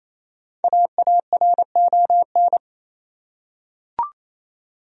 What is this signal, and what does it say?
Morse code, heard as a keyed tone, about 28 wpm, 700 Hz. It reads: AARON
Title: USA Common Names - Max 5 Letters - 28wpm